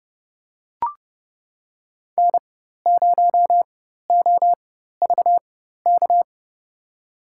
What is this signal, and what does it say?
Morse code, heard as a keyed tone, about 30 wpm, spaced as for 14 wpm, 700 Hz. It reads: N0OVK
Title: USA Call Signs - 5 Characters - 4x Character Spacing - 30wpm